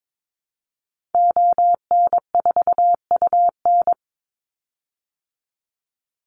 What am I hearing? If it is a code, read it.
Morse code, heard as a keyed tone, about 22 wpm, 700 Hz. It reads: ON4UN
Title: Call Signs Repeated 3x - 22wpm